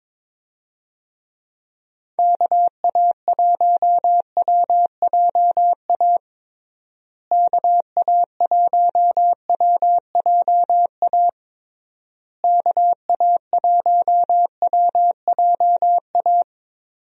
Morse code, heard as a keyed tone, about 22 wpm, 700 Hz. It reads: KA1WJA KA1WJA KA1WJA
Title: USA Call Signs - 6 Characters - Repeated 3x - 22wpm